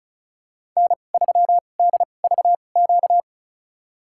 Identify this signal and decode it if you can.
Morse code, heard as a keyed tone, about 34 wpm, spaced as for 25 wpm, 700 Hz. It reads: N3DVQ
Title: USA Call Signs - 2x Character Spacing - 35wpm